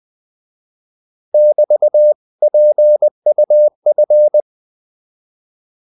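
Morse code, heard as a keyed tone, about 20 wpm, 600 Hz. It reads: <BT> PUF
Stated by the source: Morse code practice of short Danish words at 20 wpm, from CW med Gnister.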